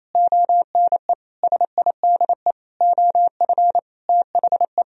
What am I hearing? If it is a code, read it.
Morse code, heard as a keyed tone, about 32 wpm, 700 Hz. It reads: ONE SIDE OF THE
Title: Speed-Racing - Sets of 5 Words 28wpm